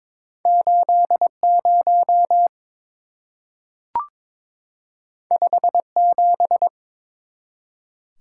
Morse code, heard as a keyed tone, about 22 wpm, 700 Hz. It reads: 80 57
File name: practice / 22wpm / Two Letter-Number Combinations - 22wpm